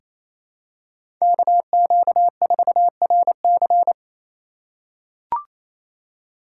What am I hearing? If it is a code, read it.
Morse code, heard as a keyed tone, about 28 wpm, 700 Hz. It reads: KQ4RC
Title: USA Call Signs - 5 Characters - 2x Character Spacing - 28wpm